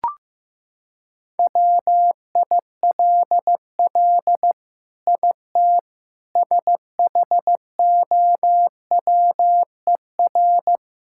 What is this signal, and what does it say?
Morse code, heard as a keyed tone, about 15 wpm, 700 Hz. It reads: WILL IT SHOWER
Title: Sentences from Top 1600 Words 15wpm